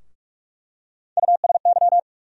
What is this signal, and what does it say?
Morse code, heard as a keyed tone, about 46 wpm, 700 Hz. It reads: USY
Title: USA Call Sign Postfixes - 3 Letters 45wpm